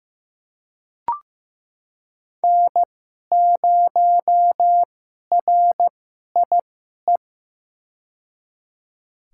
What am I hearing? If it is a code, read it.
Morse code, heard as a keyed tone, about 15 wpm, spaced as for 11 wpm, 700 Hz. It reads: N0RIE
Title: USA Call Signs - 5 Characters - 2x Character Spacing - 15wpm